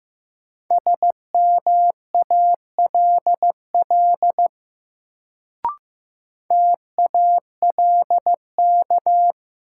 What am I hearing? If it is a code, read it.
Morse code, heard as a keyed tone, about 15 wpm, 700 Hz. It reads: SMALL TALK
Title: Sets of 2 Words - New Word Drill - 15wpm